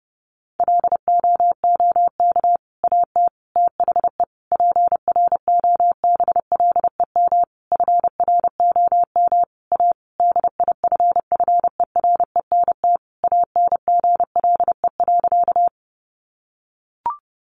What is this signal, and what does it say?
Morse code, heard as a keyed tone, about 28 wpm, 700 Hz. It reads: LOOK AT THE PROBLEM FROM A DIFFERENT ANGLE.
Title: Sentences from Top 1500 Words 30wpm